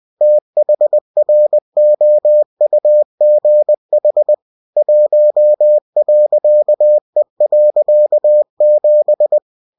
Morse code, heard as a keyed tone, about 20 wpm, 600 Hz. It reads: THROUGH 1.E.7
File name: winnie-20wpm